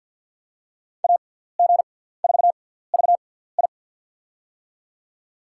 Morse code, heard as a keyed tone, about 47 wpm, spaced as for 17 wpm, 700 Hz. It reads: AG4VI
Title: USA Call Signs - 5 Characters - 6x Character Spacing - 50wpm